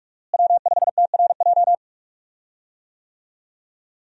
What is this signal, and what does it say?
Morse code, heard as a keyed tone, about 46 wpm, 700 Hz. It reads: W5TRJ
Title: USA Call Signs - 5 Characters - Repeated 3x - 45wpm